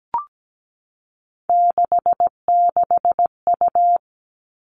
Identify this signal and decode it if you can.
Morse code, heard as a keyed tone, about 17 wpm, 700 Hz. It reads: 66U
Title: All Cut Numbers - 3-Digit 17wpm